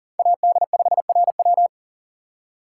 Morse code, heard as a keyed tone, about 40 wpm, 700 Hz. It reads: AD5RW
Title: USA Call Signs - 6x Character Spacing - 40wpm